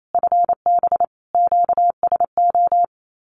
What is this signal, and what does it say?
Morse code, heard as a keyed tone, about 28 wpm, 700 Hz. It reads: FB QSO
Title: Common QSO Phrases Encore 28wpm